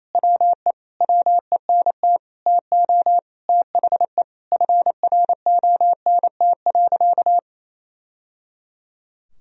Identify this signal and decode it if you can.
Morse code, heard as a keyed tone, about 28 wpm, 700 Hz. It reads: WE WENT TO THE FRONT.
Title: Sentences from Top 1000 Words - Review Part 2 - 28wpm